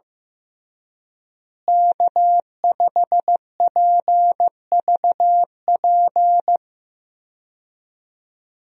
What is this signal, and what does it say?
Morse code, heard as a keyed tone, about 15 wpm, 700 Hz. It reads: K5PVP
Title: USA Call Signs - 5 Characters - Repeated 3x - 15wpm